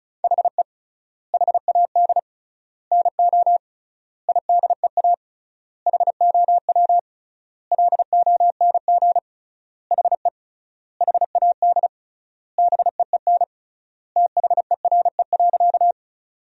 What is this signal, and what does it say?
Morse code, heard as a keyed tone, about 36 wpm, 700 Hz. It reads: HE HAD NO IDEA HOW LONG HE HAD BEEN THERE.
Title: 3x Word-Spacing - Sentences from Top 300 Words - 35wpm